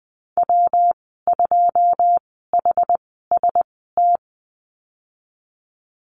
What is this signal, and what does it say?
Morse code, heard as a keyed tone, about 20 wpm, spaced as for 15 wpm, 700 Hz. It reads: W2HST
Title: 122 Lesson 20g - Number 3 - Call Signs 20wpm